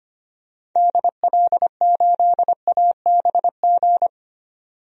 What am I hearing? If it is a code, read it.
Morse code, heard as a keyed tone, about 25 wpm, 700 Hz. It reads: DL8ABG
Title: DXCC-100 Call Signs - 6 Characters - 25wpm